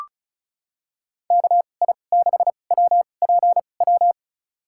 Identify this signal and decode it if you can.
Morse code, heard as a keyed tone, about 34 wpm, spaced as for 25 wpm, 700 Hz. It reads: KI6WPW
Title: USA Call Signs - 6 Characters - 2x Character Spacing - 35wpm